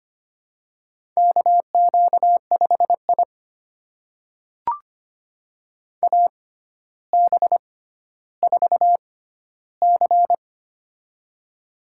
Morse code, heard as a keyed tone, about 25 wpm, spaced as for 13 wpm, 700 Hz. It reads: KQ5I AB4C
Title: USA Call Signs - 4 Characters - 6x Character Spacing - 25wpm